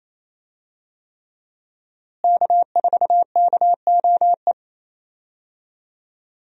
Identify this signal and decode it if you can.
Morse code, heard as a keyed tone, about 28 wpm, 700 Hz. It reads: K4KOE
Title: USA Call Signs - 5 Characters - Repeated 3x - 28wpm